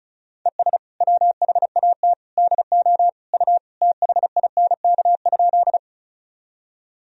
Morse code, heard as a keyed tone, about 34 wpm, 700 Hz. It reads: ES WHAT DO U THINK?
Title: Sentences from Top 200 Words and Abbreviations 35wpm